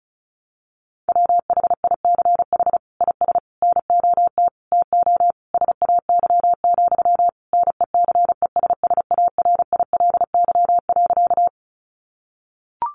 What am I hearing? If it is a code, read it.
Morse code, heard as a keyed tone, about 34 wpm, 700 Hz. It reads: WHICH IS NOT TO SAY, NECESSARILY.
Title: Sentences from Top 1400 Words 35wpm